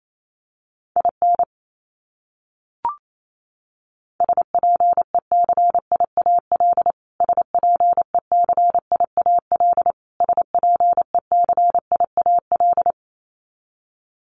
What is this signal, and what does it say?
Morse code, heard as a keyed tone, about 28 wpm, 700 Hz. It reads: IN SPECIAL SPECIAL SPECIAL
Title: Top 500 Words Repeated 28wpm